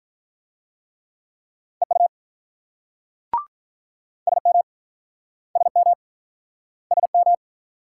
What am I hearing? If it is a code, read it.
Morse code, heard as a keyed tone, about 48 wpm, 700 Hz. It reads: EU SM SM SM